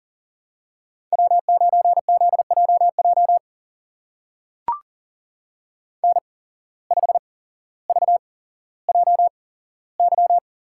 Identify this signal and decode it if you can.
Morse code, heard as a keyed tone, about 41 wpm, spaced as for 17 wpm, 700 Hz. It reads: W9ZJJ N5VJY
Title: USA Call Signs - 5 Characters - 8x Character Spacing - 40wpm